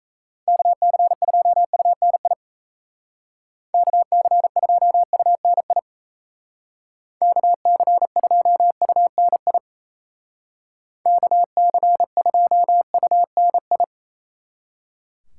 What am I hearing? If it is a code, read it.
Morse code, heard as a keyed tone, about 34 wpm, 700 Hz. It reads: KC2UNI KC2UNI KC2UNI KC2UNI
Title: Speed Racing - USA 2x3 Call Signs 28wpm